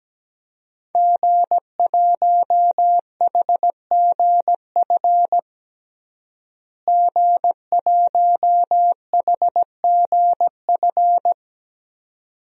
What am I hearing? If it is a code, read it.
Morse code, heard as a keyed tone, about 17 wpm, 700 Hz. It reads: G1HGF G1HGF
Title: DXCC-100 Call Signs - 5 Characters - Repeated 2x - 17wpm